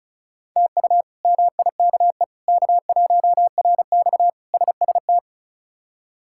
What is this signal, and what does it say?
Morse code, heard as a keyed tone, about 36 wpm, 700 Hz. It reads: TU MIKE K1RX SST
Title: K1USN SST QSO Practice - 35wpm